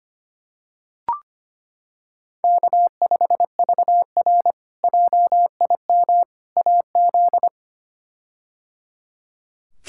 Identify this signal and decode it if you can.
Morse code, heard as a keyed tone, about 25 wpm, 700 Hz. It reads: K5VR JIM AZ
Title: K1USN SST QSO Practice - 25wpm